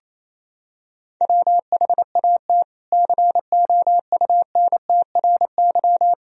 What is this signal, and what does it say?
Morse code, heard as a keyed tone, about 28 wpm, 700 Hz. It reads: WHAT COUNTRY
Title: Sentences from Top 300 Words and Abbreviations 28wpm